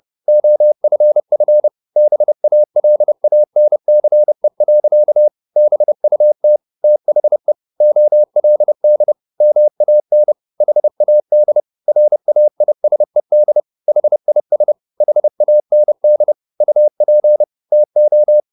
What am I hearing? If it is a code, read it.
Morse code, heard as a keyed tone, about 30 wpm, 600 Hz. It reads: OFF BALANCE. BUT THE OLD MAN HAD RAISED HIS HAND UP TO
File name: the old man and the sea 2 30wpm